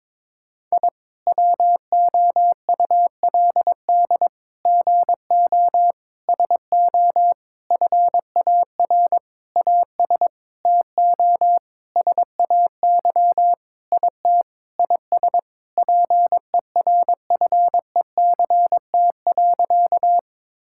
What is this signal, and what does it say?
Morse code, heard as a keyed tone, about 22 wpm, 700 Hz. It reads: I WOULD GO SO FAR AS TO SAY IT IS PERFECT.
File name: practice / 22wpm / Sentences with 8-Word Phrases - 22wpm